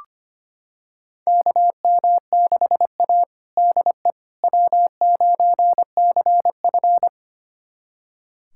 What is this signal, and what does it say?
Morse code, heard as a keyed tone, about 25 wpm, 700 Hz. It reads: KM6A DE W9CF
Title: USA Callsign DE Callsign - 4 Characters - 25wpm